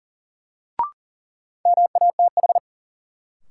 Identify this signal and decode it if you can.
Morse code, heard as a keyed tone, about 40 wpm, 700 Hz. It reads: MATH
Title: Sentences from Top 1000 Words - Review Part 6 - New Word Drill - 40wpm